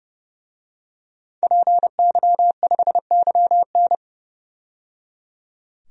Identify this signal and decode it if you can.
Morse code, heard as a keyed tone, about 30 wpm, 700 Hz. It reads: PY5YN
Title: DXCC-100 Call Signs - 5 Characters - 30wpm